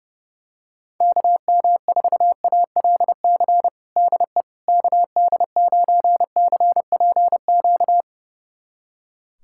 Morse code, heard as a keyed tone, about 32 wpm, 700 Hz. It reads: KM4ALC DE KD9CPQ
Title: USA Callsign DE Callsign 30wpm